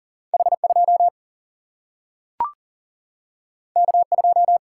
Morse code, heard as a keyed tone, about 40 wpm, 700 Hz. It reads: H 2 K 2
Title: Single-Letter Single-Number 40wpm